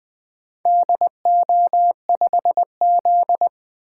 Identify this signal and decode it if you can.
Morse code, heard as a keyed tone, about 20 wpm, 700 Hz. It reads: DO5Z
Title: Four Letter-Number Combinations - 20wpm